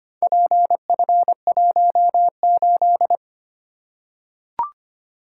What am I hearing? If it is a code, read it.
Morse code, heard as a keyed tone, about 25 wpm, 700 Hz. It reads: PF18
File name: practice / 25wpm / Four Letter-Number Combinations - 25wpm